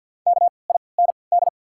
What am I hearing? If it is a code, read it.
Morse code, heard as a keyed tone, about 48 wpm, 690 Hz. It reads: K I N D
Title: 3x Character Spacing - Top 200 Words - 50wpm